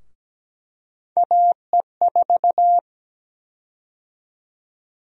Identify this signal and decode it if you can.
Morse code, heard as a keyed tone, about 17 wpm, 700 Hz. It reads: AE4
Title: USA Call Sign Prefixes - 17wpm